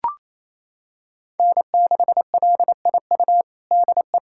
Morse code, heard as a keyed tone, about 28 wpm, 700 Hz. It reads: N6LIU DE
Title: USA Callsign DE Callsign - 5 Characters - 28wpm